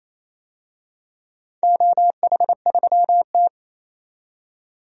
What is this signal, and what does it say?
Morse code, heard as a keyed tone, about 28 wpm, 700 Hz. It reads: OH3T
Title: DXCC-100 Call Signs - 4 Characters - 28wpm